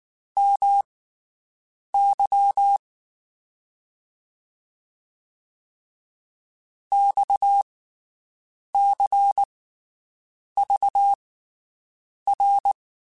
Morse code, heard as a keyed tone, about 19 wpm, spaced as for 7 wpm, 800 Hz. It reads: MY XCVR
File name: QSOs Spaced